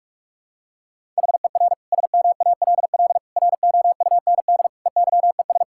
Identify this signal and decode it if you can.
Morse code, heard as a keyed tone, about 46 wpm, 700 Hz. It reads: HER SMALL ROUND EYES
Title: Sentences from Top 1000 Words - Review Part 6 - 45wpm